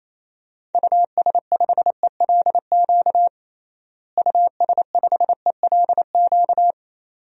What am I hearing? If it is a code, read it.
Morse code, heard as a keyed tone, about 28 wpm, 700 Hz. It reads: US5ELQ US5ELQ
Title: DXCC-100 Call Signs - 6 Characters - Repeated 2x - 28wpm